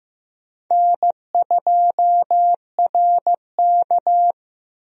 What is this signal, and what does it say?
Morse code, heard as a keyed tone, about 15 wpm, 700 Hz. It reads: N2RK
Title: USA Call Signs - 4 Characters - 15wpm